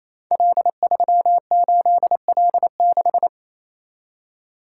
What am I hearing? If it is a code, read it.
Morse code, heard as a keyed tone, about 27 wpm, 700 Hz. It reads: L38L6
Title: Code Groups - 5 Letter:Numbers - 28wpm